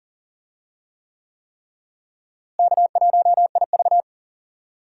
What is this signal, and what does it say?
Morse code, heard as a keyed tone, about 40 wpm, 700 Hz. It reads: K1IV
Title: USA Call Signs - 4 Characters - Repeated 3x - 40wpm